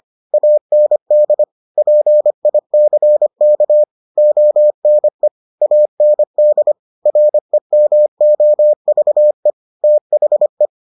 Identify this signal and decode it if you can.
Morse code, heard as a keyed tone, about 25 wpm, 600 Hz. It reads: AND PICK ONE AND REMOVE THE